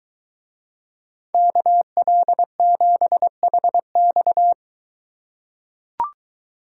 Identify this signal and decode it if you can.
Morse code, heard as a keyed tone, about 21 wpm, 700 Hz. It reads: KL7HX